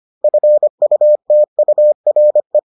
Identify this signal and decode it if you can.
Morse code, heard as a keyed tone, about 25 wpm, 600 Hz. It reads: FUTURE